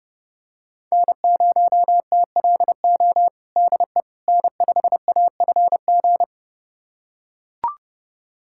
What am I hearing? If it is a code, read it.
Morse code, heard as a keyed tone, about 30 wpm, 700 Hz. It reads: N0TLO DE N5AFG
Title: USA Callsign DE Callsign - 5 Characters - 30wpm